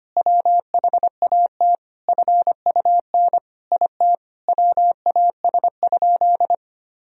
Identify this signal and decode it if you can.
Morse code, heard as a keyed tone, about 25 wpm, 700 Hz. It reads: WHAT FUN IT WAS?
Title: Sentences from Top 1000 Words - Review Part 4 - 25wpm